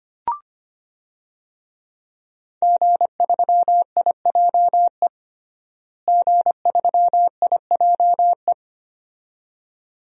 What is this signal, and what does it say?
Morse code, heard as a keyed tone, about 25 wpm, 700 Hz. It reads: G3IJE G3IJE
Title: DXCC-100 Call Signs - 5 Characters - Repeated 2x - 25wpm